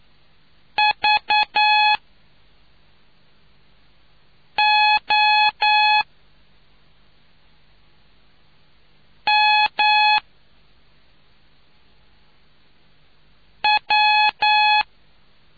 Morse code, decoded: V O M W

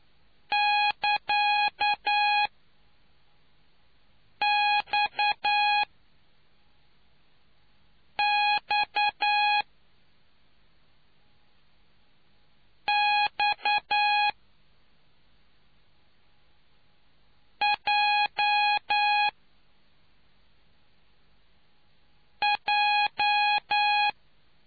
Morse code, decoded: <KA>XXXJJ

